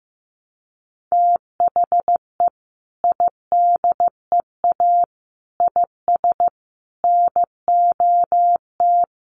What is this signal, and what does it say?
Morse code, heard as a keyed tone, about 15 wpm, 700 Hz. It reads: THE IDEA IS NOT
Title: Sentences from Top 1300 Words 15wpm